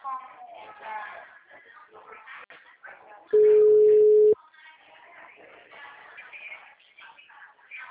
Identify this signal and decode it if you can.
Ringback tone (Europe, 425 Hz), with speech.